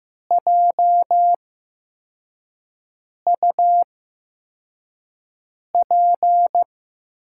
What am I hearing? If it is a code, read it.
Morse code, heard as a keyed tone, about 15 wpm, spaced as for 4 wpm, 700 Hz. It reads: JUP
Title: USA Call Signs - 6 Characters - 8x Character Spacing - 15wpm